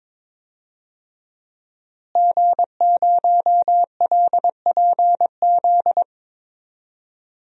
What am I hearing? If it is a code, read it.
Morse code, heard as a keyed tone, about 22 wpm, 700 Hz. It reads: G0LPZ
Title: DXCC-100 Call Signs - 5 Characters - Repeated 3x - 22wpm